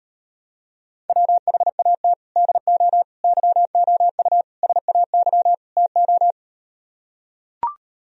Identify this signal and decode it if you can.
Morse code, heard as a keyed tone, about 38 wpm, 700 Hz. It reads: WHAT DO YOU SAY TO